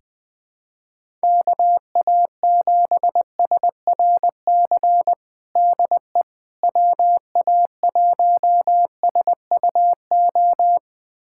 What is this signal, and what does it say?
Morse code, heard as a keyed tone, about 20 wpm, 700 Hz. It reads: KA7SRC DE WA1SUO
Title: USA Callsign DE Callsign - 6 Characters - 20wpm